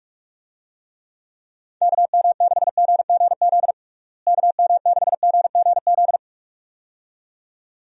Morse code, heard as a keyed tone, about 41 wpm, 700 Hz. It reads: KM6GGZ KM6GGZ
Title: USA Call Signs - 6 Characters - Repeated 2x - 45wpm